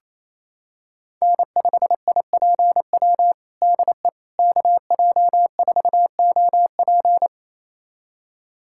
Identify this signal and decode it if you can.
Morse code, heard as a keyed tone, about 28 wpm, 700 Hz. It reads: N5IPW DE KJ4OP
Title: USA Callsign DE Callsign - 5 Characters - 28wpm